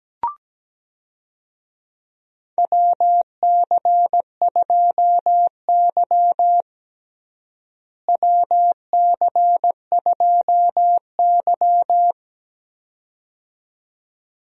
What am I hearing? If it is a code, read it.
Morse code, heard as a keyed tone, about 17 wpm, 700 Hz. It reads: WC2Y WC2Y
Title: USA Call Signs - 4 Characters - Repeated 2x - 17wpm